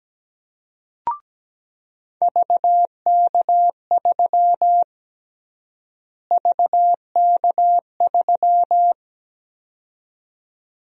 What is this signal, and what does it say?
Morse code, heard as a keyed tone, about 17 wpm, 700 Hz. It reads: VK3 VK3
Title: USA Call Sign Prefixes - Repeated 2x - 17wpm